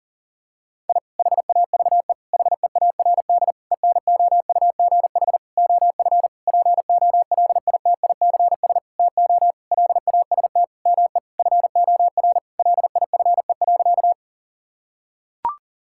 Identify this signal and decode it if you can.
Morse code, heard as a keyed tone, about 37 wpm, 700 Hz. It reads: I HAVE HEARD ENOUGH OF POLITICS TO LAST ME FOR LIFE.
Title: Sentences from Top 1000 Words - Review Part 6 - 40wpm